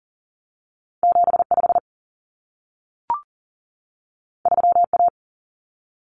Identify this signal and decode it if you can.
Morse code, heard as a keyed tone, about 40 wpm, 700 Hz. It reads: 75 3A
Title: Common Cut Numbers - 2-Digit 40wpm